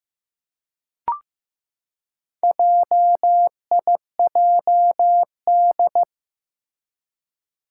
Morse code, heard as a keyed tone, about 15 wpm, 700 Hz. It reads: JIJD
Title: Four Challenging Letter-Number Combinations 15wpm